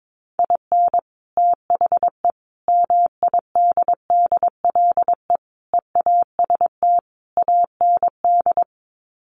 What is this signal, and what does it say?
Morse code, heard as a keyed tone, about 22 wpm, 700 Hz. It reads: IN THE MIDDLE EAST AND
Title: Sets of 5 Words Encore 22wpm